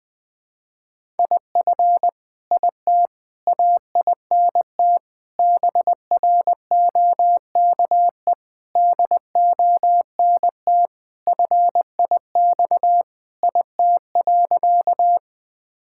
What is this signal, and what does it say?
Morse code, heard as a keyed tone, about 21 wpm, 700 Hz. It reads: IF IT AINT BROKE DONT FIX IT.
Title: Sentences with 7-Word Phrases - Spoken Beginning - 20wpm